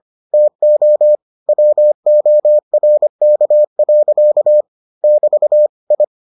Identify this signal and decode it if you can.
Morse code, heard as a keyed tone, about 25 wpm, 600 Hz. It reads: TO WORK. <BT> I